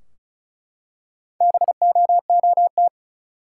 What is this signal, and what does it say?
Morse code, heard as a keyed tone, about 36 wpm, 700 Hz. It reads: BOOT
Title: Four Letter Words - 35wpm